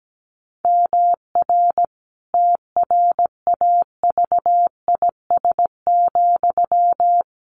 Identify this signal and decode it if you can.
Morse code, heard as a keyed tone, about 17 wpm, 700 Hz. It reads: MR TRAVIS,
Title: A Sound Of Thunder 17wpm